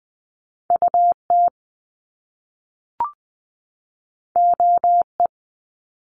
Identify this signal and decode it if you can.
Morse code, heard as a keyed tone, about 20 wpm, 700 Hz. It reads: UT OE
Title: Top Two Letter Combinations in Words 20wpm